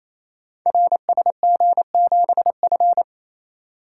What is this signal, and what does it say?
Morse code, heard as a keyed tone, about 27 wpm, 700 Hz. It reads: RSG7F
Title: Code Groups - 5 Letter:Numbers - 28wpm